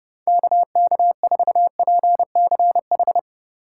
Morse code, heard as a keyed tone, about 38 wpm, 700 Hz. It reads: KK4PCH